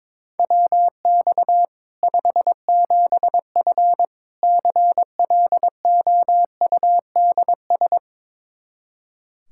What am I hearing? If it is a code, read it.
Morse code, heard as a keyed tone, about 22 wpm, 700 Hz. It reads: WX 57F CLOUDS
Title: K1USN SST QSO Practice 22wpm